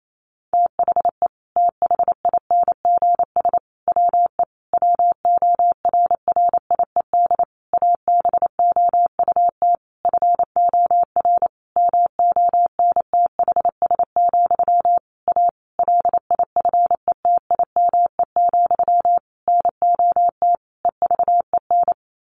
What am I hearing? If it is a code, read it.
Morse code, heard as a keyed tone, about 28 wpm, 700 Hz. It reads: THE THINGS WE WORRIED ABOUT FOR MONTHS, A LIFETIME, NOT EVEN
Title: A Sound Of Thunder 28wpm